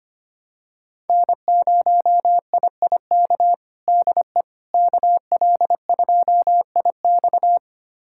Morse code, heard as a keyed tone, about 25 wpm, 700 Hz. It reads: N0IIK DE KL2IX
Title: USA Callsign DE Callsign - 5 Characters - 25wpm